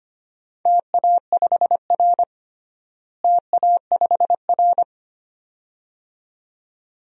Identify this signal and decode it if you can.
Morse code, heard as a keyed tone, about 25 wpm, 700 Hz. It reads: TA5R TA5R
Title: DXCC-100 Call Signs - 4 Characters - Repeated 2x - 25wpm